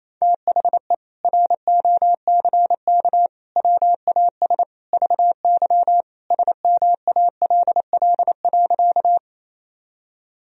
Morse code, heard as a keyed tone, about 28 wpm, 700 Hz. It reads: THE ROCK WAS VY SMALL.
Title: Sentences from Top 400 Words and Abbreviations 28wpm